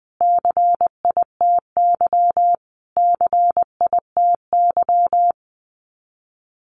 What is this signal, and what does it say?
Morse code, heard as a keyed tone, about 20 wpm, 700 Hz. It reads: CITY CITY
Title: Top 300 Words Repeated 20wpm